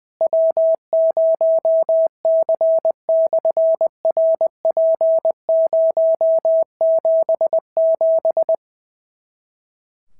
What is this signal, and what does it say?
Morse code, heard as a keyed tone, about 20 wpm, 650 Hz. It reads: W0C/RP077
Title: SOTA QSO Practice 20wpm